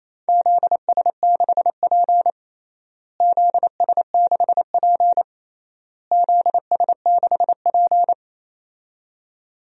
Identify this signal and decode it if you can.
Morse code, heard as a keyed tone, about 28 wpm, 700 Hz. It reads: ZS6P ZS6P ZS6P
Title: DXCC-100 Call Signs - 4 Characters - Repeated 3x - 28wpm